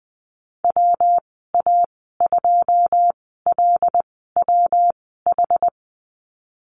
Morse code, heard as a keyed tone, about 20 wpm, spaced as for 15 wpm, 700 Hz. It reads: WA2LWH
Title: 147 Lesson 24g - Letter y - Call Signs 20wpm